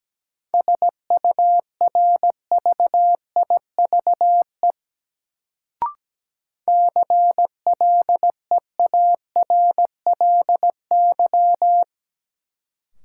Morse code, heard as a keyed tone, about 17 wpm, 700 Hz. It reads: SURVIVE CLEARLY
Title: Sentences from Top 1300 Words - New Word Drill - 17wpm